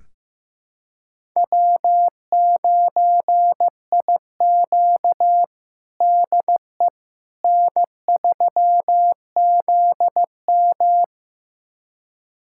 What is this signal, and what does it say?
Morse code, heard as a keyed tone, about 15 wpm, 700 Hz. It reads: W9IQ DE N3ZM
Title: USA Callsign DE Callsign - 4 Characters - 15wpm